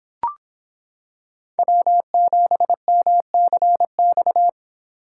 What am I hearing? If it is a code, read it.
Morse code, heard as a keyed tone, about 26 wpm, 700 Hz. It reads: W7MCX